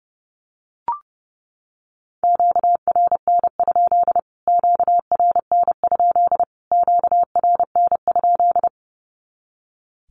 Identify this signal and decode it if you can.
Morse code, heard as a keyed tone, about 30 wpm, 700 Hz. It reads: QRN? QRN? QRN?
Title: Q-Codes Repeated 3x 30wpm